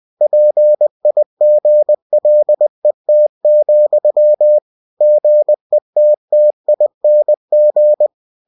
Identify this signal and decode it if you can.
Morse code, heard as a keyed tone, about 20 wpm, 600 Hz. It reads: PIGLET, GETTING
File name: cw-books / winnie-the-pooh / winnie-20wpm-page-0083